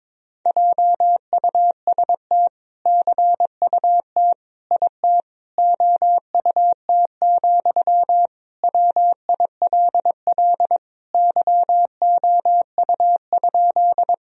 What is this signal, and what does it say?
Morse code, heard as a keyed tone, about 22 wpm, 700 Hz. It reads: JUST CUT IT OUT, WILL YOU?
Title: Sentences from Top 1000 Words - Review Part 1 - 22wpm